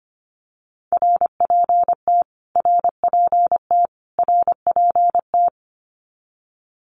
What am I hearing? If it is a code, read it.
Morse code, heard as a keyed tone, about 25 wpm, 700 Hz. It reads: RPT RPT RPT
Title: Abbreviations Repeated 3x - 25wpm